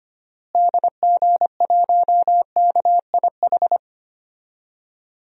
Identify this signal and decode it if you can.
Morse code, heard as a keyed tone, about 25 wpm, 700 Hz. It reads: DG1KIH
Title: DXCC-100 Call Signs - 6 Characters - Repeated 3x - 25wpm